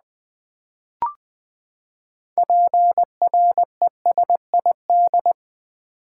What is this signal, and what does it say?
Morse code, heard as a keyed tone, about 20 wpm, 700 Hz. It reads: PRESID
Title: Build A Word - 6 Letters At A Time - 12-Letter Words - 20wpm